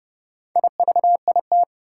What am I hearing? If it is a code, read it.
Morse code, heard as a keyed tone, about 30 wpm, 700 Hz. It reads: IVIT
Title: Four Challenging Letter-Number Combinations 30wpm